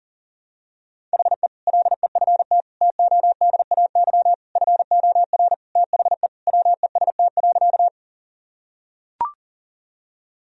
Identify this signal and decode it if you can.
Morse code, heard as a keyed tone, about 40 wpm, 700 Hz. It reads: HE LEFT TODAY FOR THE WEST.
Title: Sentences from Top 1000 Words - Review Part 2 - 40wpm